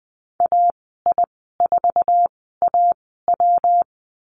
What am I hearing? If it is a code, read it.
Morse code, heard as a keyed tone, about 20 wpm, spaced as for 14 wpm, 700 Hz. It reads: AI4AW
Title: 140 Lesson 23g - Letter f - Call Signs 20wpm